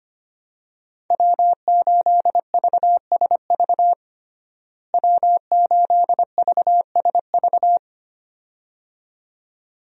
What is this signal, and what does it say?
Morse code, heard as a keyed tone, about 25 wpm, 700 Hz. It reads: W8VSV W8VSV
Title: USA Call Signs - 5 Characters - Repeated 2x - 25wpm